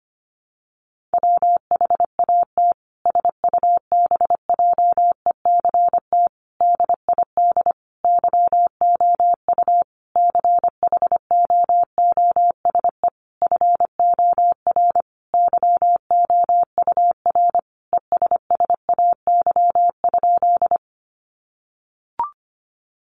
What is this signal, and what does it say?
Morse code, heard as a keyed tone, about 25 wpm, 700 Hz. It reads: WHAT SUBJECT DID YOU CHOOSE FOR YOUR ESSAY?
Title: Sentences from Top 1500 Words 25wpm